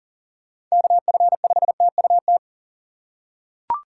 Morse code, heard as a keyed tone, about 37 wpm, 700 Hz. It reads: KF5TUT
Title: USA Call Signs - 6 Characters - 3x Character Spacing - 40wpm